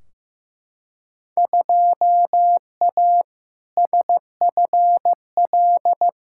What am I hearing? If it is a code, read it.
Morse code, heard as a keyed tone, about 15 wpm, 700 Hz. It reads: 2A SFL
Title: Field Day Exchanges Encore - 15wpm